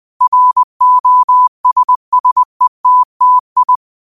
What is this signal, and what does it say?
Morse code, heard as a keyed tone, about 20 wpm, 1000 Hz. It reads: ROSSETTI